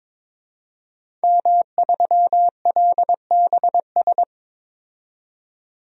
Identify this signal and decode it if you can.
Morse code, heard as a keyed tone, about 21 wpm, 700 Hz. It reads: M3LBS
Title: DXCC-100 Call Signs - 5 Characters - 22wpm